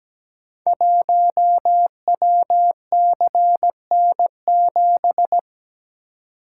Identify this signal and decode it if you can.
Morse code, heard as a keyed tone, about 17 wpm, 690 Hz. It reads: 1WCN7
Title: Code Groups - 5 Letter:Numbers - 17wpm